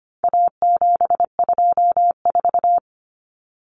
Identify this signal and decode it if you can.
Morse code, heard as a keyed tone, about 25 wpm, 700 Hz. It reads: A724
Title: Common Cut Numbers - 4-Digit 25wpm